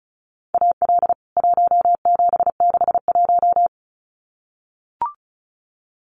Morse code, heard as a keyed tone, about 34 wpm, 700 Hz. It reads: AL 1761
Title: CWT Exchange 35wpm